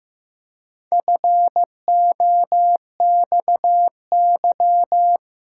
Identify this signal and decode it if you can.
Morse code, heard as a keyed tone, about 15 wpm, 700 Hz. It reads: FOXY
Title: Pangram Challenge v2 - 15wpm